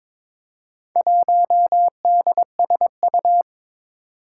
Morse code, heard as a keyed tone, about 22 wpm, 700 Hz. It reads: 1DSU
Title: Four Challenging Letter-Number Combinations 22wpm